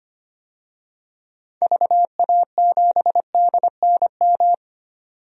Morse code, heard as a keyed tone, about 25 wpm, 700 Hz. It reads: VA7DNM
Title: DXCC-100 Call Signs - 6 Characters - 25wpm